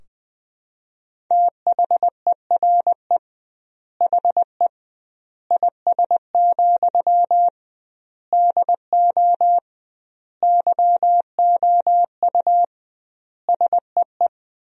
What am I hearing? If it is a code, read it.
Morse code, heard as a keyed tone, about 20 wpm, 700 Hz. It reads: THERE HE IS, DO YOU SEE
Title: 2x Word-Spacing - Sentences from Top 100 Words - 20wpm